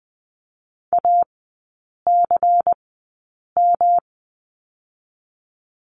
Morse code, heard as a keyed tone, about 19 wpm, 700 Hz. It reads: A C M
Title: 114 Lesson 19e - Letter w - Three Characters 20wpm